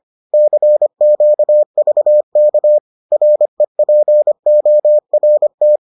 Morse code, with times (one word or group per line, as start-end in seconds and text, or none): 0.3-2.8 CQVK
3.1-5.8 REPORT